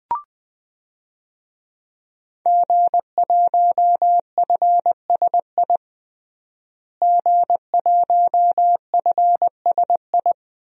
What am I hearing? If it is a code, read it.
Morse code, heard as a keyed tone, about 20 wpm, 700 Hz. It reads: G1FSI G1FSI
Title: DXCC-100 Call Signs - 5 Characters - Repeated 2x - 20wpm